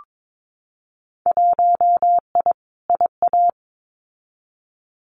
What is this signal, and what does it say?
Morse code, heard as a keyed tone, about 22 wpm, 700 Hz. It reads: 1I IA
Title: Winter Field Day Exchanges 22wpm